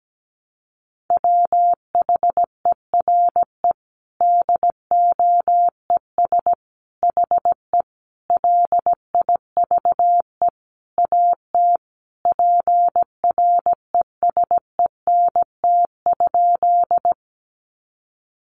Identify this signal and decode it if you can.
Morse code, heard as a keyed tone, about 17 wpm, 700 Hz. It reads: WHERE DOES HE LIVE AT PRESENT?
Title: Sentences from Top 800 Words 17wpm